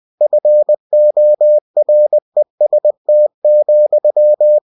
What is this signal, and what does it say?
Morse code, heard as a keyed tone, about 20 wpm, 600 Hz. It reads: FOREST,
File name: cw-books / winnie-the-pooh / winnie-20wpm-page-0002